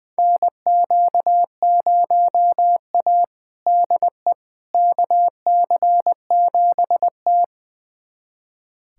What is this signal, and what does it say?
Morse code, heard as a keyed tone, about 20 wpm, 700 Hz. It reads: NQ0A DE KC7T
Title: USA Callsign DE Callsign - 4 Characters - 20wpm